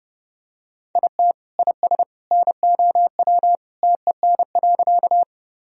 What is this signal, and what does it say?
Morse code, heard as a keyed tone, about 30 wpm, 700 Hz. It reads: IT IS NOW TEN.
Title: Sentences from Top 1000 Words - Review Part 2 - 30wpm